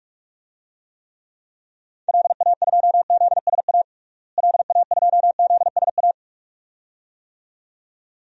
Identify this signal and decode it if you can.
Morse code, heard as a keyed tone, about 47 wpm, 700 Hz. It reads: LA2ZSA LA2ZSA
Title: DXCC-100 Call Signs - 6 Characters - Repeated 2x - 45wpm